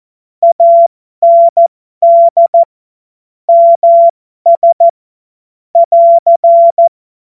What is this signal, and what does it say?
Morse code, heard as a keyed tone, about 14 wpm, spaced as for 12 wpm, 680 Hz. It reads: AND MS <AR>